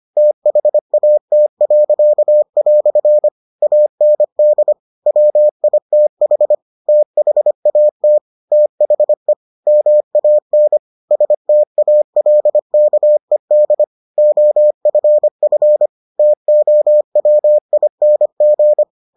Morse code, heard as a keyed tone, about 25 wpm, 600 Hz. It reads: THAT." AND WITH THAT THE MAN STALKED OFF TOWING